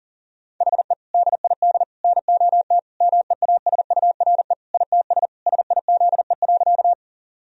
Morse code, heard as a keyed tone, about 39 wpm, 700 Hz. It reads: HE DID NOT MEASURE ITS SIZE.
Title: Sentences from Top 1000 Words - Review Part 2 - 40wpm